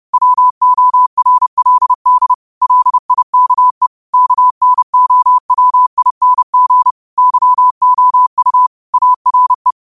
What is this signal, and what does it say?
Morse code, heard as a keyed tone, about 30 wpm, 1000 Hz. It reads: WORLD LIKE KNOWING YOU ARE
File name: qotdfiles / 30wpm